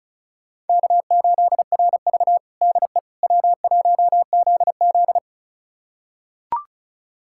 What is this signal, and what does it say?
Morse code, heard as a keyed tone, about 36 wpm, 700 Hz. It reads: K8RV DE W1ZZ
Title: USA Callsign DE Callsign - 4 Characters - 35wpm